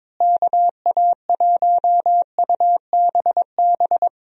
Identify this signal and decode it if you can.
Morse code, heard as a keyed tone, about 22 wpm, 700 Hz. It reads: KA1UBB
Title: USA Call Signs - 6 Characters - Repeated 2x - 22wpm